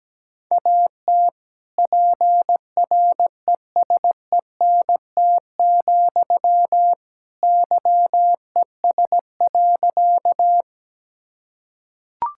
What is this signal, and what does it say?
Morse code, heard as a keyed tone, about 17 wpm, 700 Hz. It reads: AT PRESENT, YES.
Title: Sentences from Top 1000 Words - Review Part 5 - 17wpm